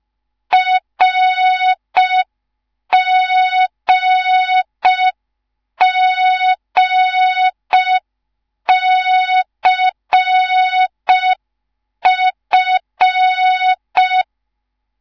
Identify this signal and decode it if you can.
Morse code, heard as a keyed tone, about 5 wpm, 3700 Hz. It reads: RGGCF